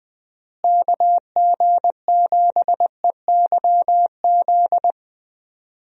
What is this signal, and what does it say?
Morse code, heard as a keyed tone, about 20 wpm, 700 Hz. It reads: KG7EYZ
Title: USA Call Signs - 6 Characters - Repeated 3x - 20wpm